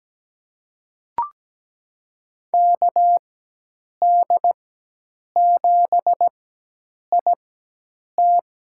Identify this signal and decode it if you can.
Morse code, heard as a keyed tone, about 17 wpm, spaced as for 8 wpm, 700 Hz. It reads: KD7IT
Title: USA Call Signs - 5 Characters - 4x Character Spacing - 17wpm